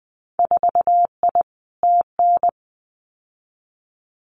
Morse code, heard as a keyed tone, about 20 wpm, 700 Hz. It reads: 4I TN
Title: Winter Field Day Exchanges 20wpm